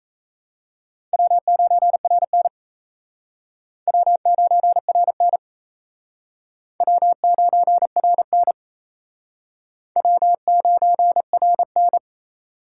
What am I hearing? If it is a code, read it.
Morse code, heard as a keyed tone, about 36 wpm, 700 Hz. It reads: W9RN W9RN W9RN W9RN